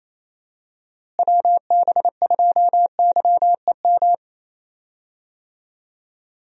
Morse code, heard as a keyed tone, about 27 wpm, 700 Hz. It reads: WB2YEM
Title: USA Call Signs - 6 Characters - Repeated 3x - 28wpm